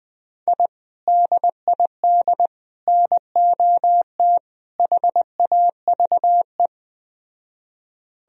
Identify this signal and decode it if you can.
Morse code, heard as a keyed tone, about 20 wpm, 700 Hz. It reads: I DID NOT HAVE